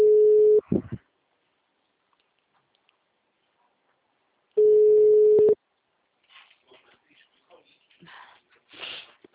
Ringback tone (Europe, 425 Hz).
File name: out- -2016- - - .6492